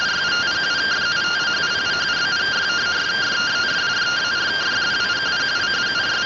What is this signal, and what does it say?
Two-tone teleprinter signal, about 170 Hz shift.